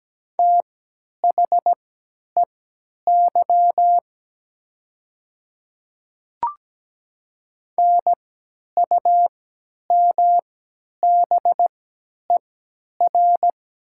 Morse code, heard as a keyed tone, about 17 wpm, spaced as for 10 wpm, 700 Hz. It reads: THEY NUMBER
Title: 3x Character Spacing - Top 300 Words - 17wpm